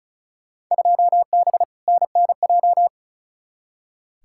Morse code, heard as a keyed tone, about 36 wpm, 700 Hz. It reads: 2B NNJ
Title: Field Day Exchanges Encore - 35wpm